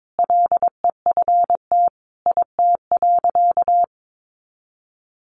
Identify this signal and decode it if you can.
Morse code, heard as a keyed tone, about 22 wpm, 700 Hz. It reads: LEFT IT.
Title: A Sound Of Thunder 22wpm